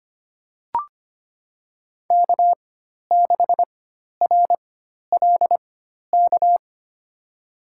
Morse code, heard as a keyed tone, about 25 wpm, spaced as for 12 wpm, 700 Hz. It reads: K6RLK
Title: USA Call Signs - 5 Characters - 4x Character Spacing - 25wpm